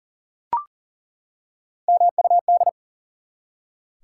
Morse code, heard as a keyed tone, about 40 wpm, 700 Hz. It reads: MUD
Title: Three Letter Words - 40wpm